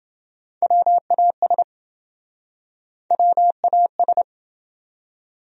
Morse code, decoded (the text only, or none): WAS WAS